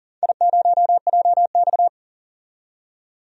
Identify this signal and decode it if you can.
Morse code, heard as a keyed tone, about 42 wpm, 700 Hz. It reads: I0JX
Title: DXCC-100 Call Signs - 4 Characters - 40wpm